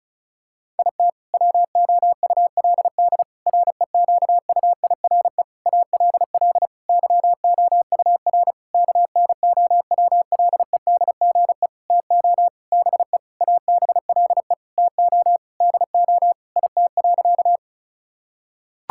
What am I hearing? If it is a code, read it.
Morse code, heard as a keyed tone, about 34 wpm, 700 Hz. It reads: IT WOULD REQUIRE ALL YOUR KNOWLEDGE TO BE ABLE TO DO IT.
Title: Sentences from Top 1000 Words - Review Part 6 - 35wpm